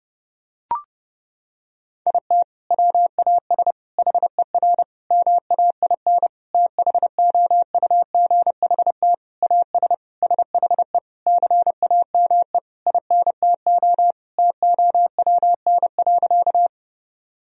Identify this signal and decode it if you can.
Morse code, heard as a keyed tone, about 30 wpm, 700 Hz. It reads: IT WAS HER MAIN THOUGHT AS SHE CAME INTO TOWN.
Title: Sentences from Top 1000 Words - Review Part 2 - 30wpm